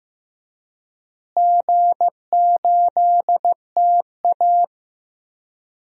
Morse code, heard as a keyed tone, about 15 wpm, 700 Hz. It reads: G8TA
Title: DXCC-100 Call Signs - 4 Characters - Repeated 3x - 15wpm